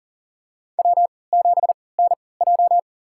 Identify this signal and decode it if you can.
Morse code, heard as a keyed tone, about 40 wpm, spaced as for 23 wpm, 700 Hz. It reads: W7NJ
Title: USA Call Signs - 4 Characters - 3x Character Spacing - 40wpm